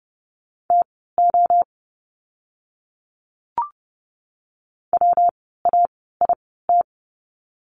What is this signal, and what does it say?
Morse code, heard as a keyed tone, about 30 wpm, spaced as for 17 wpm, 700 Hz. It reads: TO WAIT
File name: course / 30:10wpm / 115 Lesson 19f - Letter w - Words 30wpm